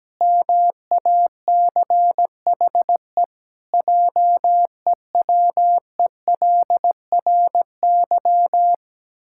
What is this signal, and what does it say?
Morse code, heard as a keyed tone, about 17 wpm, 700 Hz. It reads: MACHE JEWELRY
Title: Pangram Challenge v2 - 17wpm